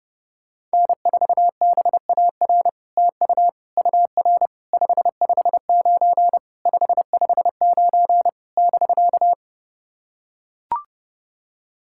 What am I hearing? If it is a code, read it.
Morse code, heard as a keyed tone, about 30 wpm, 700 Hz. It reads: N4BAR TU UR 559 559 <BK>
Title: SOTA QSO Practice 30wpm